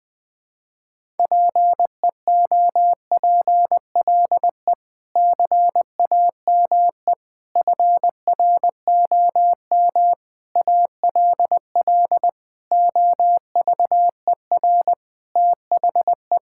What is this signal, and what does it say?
Morse code, heard as a keyed tone, about 20 wpm, 700 Hz. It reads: PEOPLE CAME FROM ALL OVER THE
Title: Sentences with 5-Word Phrases - 20wpm